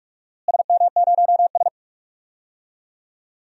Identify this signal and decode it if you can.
Morse code, heard as a keyed tone, about 41 wpm, 700 Hz. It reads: SM0S
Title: DXCC-100 Call Signs - 4 Characters - Repeated 2x - 45wpm